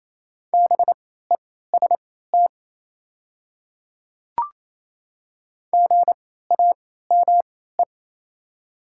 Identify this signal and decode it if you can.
Morse code, heard as a keyed tone, about 28 wpm, spaced as for 16 wpm, 700 Hz. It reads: BEST GAME
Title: 3x Character Spacing - Top 500 Words - 28wpm